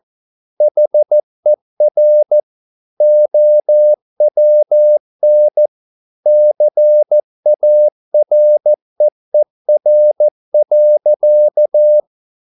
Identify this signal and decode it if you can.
Morse code, heard as a keyed tone, about 14 wpm, 600 Hz. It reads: HER OWN CAREER.